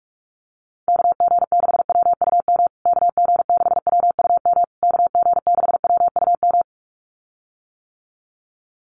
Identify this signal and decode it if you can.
Morse code, heard as a keyed tone, about 44 wpm, 700 Hz. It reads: KG6WUM KG6WUM KG6WUM
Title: USA Call Signs Repeated 3x 45wpm